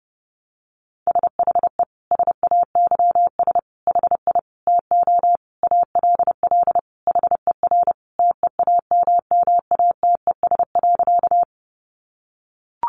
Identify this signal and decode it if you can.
Morse code, heard as a keyed tone, about 30 wpm, 700 Hz. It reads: SHE SAYS HI TO ALL HER TEAMMATES.
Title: Easy Phrases 30wpm